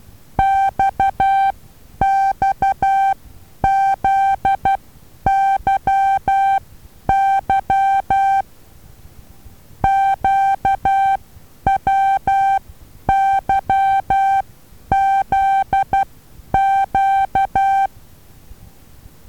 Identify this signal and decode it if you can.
Morse code, heard as a keyed tone, about 12 wpm, spaced as for 9 wpm, 780 Hz. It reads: XXZYY QWYZQ
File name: Taviro HA6VA B